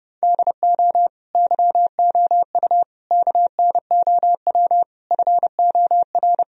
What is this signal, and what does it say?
Morse code, heard as a keyed tone, about 30 wpm, 700 Hz. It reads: DO YOU KNOW FOR
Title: Sentences from Top 1000 Words - Review Part 2 - 30wpm